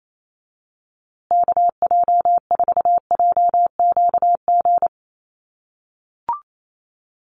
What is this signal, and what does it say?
Morse code, heard as a keyed tone, about 28 wpm, 700 Hz. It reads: KJ4JQG
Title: USA Call Signs Repeated 3x 28wpm